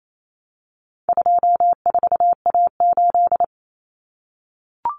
Morse code, heard as a keyed tone, about 28 wpm, 700 Hz. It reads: 24A8
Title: Common Cut Numbers - 4-Digit 28wpm